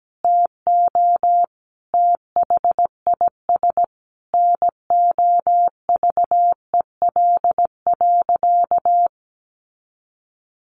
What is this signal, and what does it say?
Morse code, heard as a keyed tone, about 17 wpm, 700 Hz. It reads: TO THIS NOVEL.
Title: Sentences from Top 1300 Words 17wpm